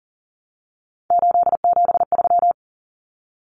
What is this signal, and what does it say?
Morse code, heard as a keyed tone, about 40 wpm, 700 Hz. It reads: 873
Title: Common Cut Numbers - 3-Digit 40wpm